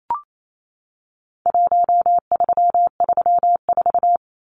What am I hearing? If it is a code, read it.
Morse code, heard as a keyed tone, about 28 wpm, 700 Hz. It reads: 1334
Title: Numbers - Contest Style - 4-Digit 28wpm